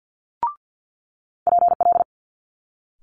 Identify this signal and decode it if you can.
Morse code, heard as a keyed tone, about 50 wpm, 700 Hz. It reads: PL
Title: Multi-letter Phonemes Quick Fire 50wpm